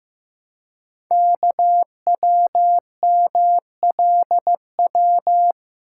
Morse code, heard as a keyed tone, about 15 wpm, 700 Hz. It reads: KWMLW
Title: Code Groups - 5 Letters - 15wpm